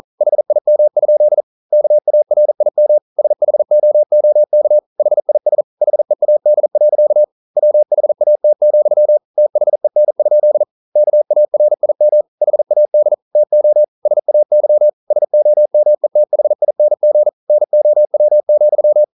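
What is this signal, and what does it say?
Morse code, heard as a keyed tone, about 40 wpm, 600 Hz. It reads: HIM? KARIM SHOOK HIS HEAD. WHAT, THEN? KARIM HAD TO SAY SOMETHING NOW,
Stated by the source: Morse code at 40 wpm, practice text, CW med Gnister.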